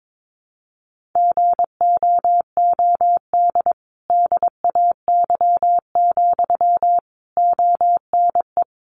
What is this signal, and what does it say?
Morse code, heard as a keyed tone, about 22 wpm, 700 Hz. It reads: GOOD DAY, ONE
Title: Sentences from Top 200 Words 22wpm